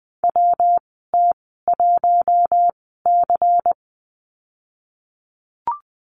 Morse code, heard as a keyed tone, about 20 wpm, spaced as for 15 wpm, 700 Hz. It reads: WT1C
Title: 173 Lesson 28g - Number 9 - Call Signs 20wpm